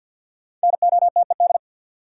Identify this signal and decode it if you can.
Morse code, heard as a keyed tone, about 53 wpm, 700 Hz. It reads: NOTED
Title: Sets of 5 Words - New Word Drill - 50wpm